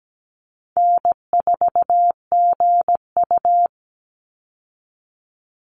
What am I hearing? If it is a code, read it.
Morse code, heard as a keyed tone, about 17 wpm, 700 Hz. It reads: N4GU
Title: Call Signs - 17wpm